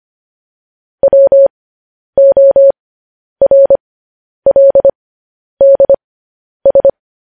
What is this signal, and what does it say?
Morse code, heard as a keyed tone, about 25 wpm, spaced as for 10 wpm, 550 Hz. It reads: WORLDS